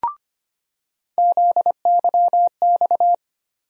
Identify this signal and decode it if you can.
Morse code, heard as a keyed tone, about 25 wpm, 700 Hz. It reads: ZYX
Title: USA Call Sign Postfixes - 3 Letters 25wpm